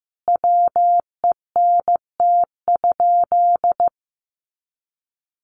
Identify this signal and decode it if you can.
Morse code, heard as a keyed tone, about 15 wpm, 700 Hz. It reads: WENT?
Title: Sentences from Top 700 Words 15wpm